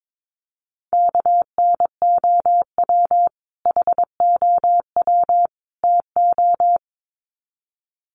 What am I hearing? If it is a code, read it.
Morse code, heard as a keyed tone, about 22 wpm, 700 Hz. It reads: KNOW HOW TO
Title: Sets of 3 Words 22wpm